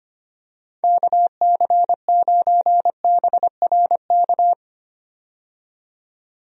Morse code, heard as a keyed tone, about 25 wpm, 700 Hz. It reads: KC9BRK
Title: USA Call Signs - 6 Characters - 25wpm